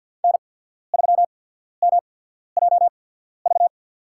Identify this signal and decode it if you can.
Morse code, heard as a keyed tone, about 53 wpm, spaced as for 14 wpm, 700 Hz. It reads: N3MJV
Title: USA Call Signs - 5 Characters - 8x Character Spacing - 50wpm